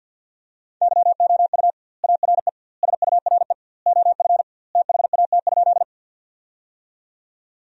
Morse code, heard as a keyed tone, about 53 wpm, 700 Hz. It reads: YOU ARE SURE OF THAT?
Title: 2x Word-Spacing - Sentences from Top 300 Words - 50wpm